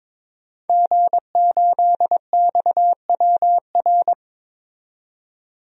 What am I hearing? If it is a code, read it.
Morse code, heard as a keyed tone, about 22 wpm, 700 Hz. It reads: G8XWR
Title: DXCC-100 Call Signs - 5 Characters - 22wpm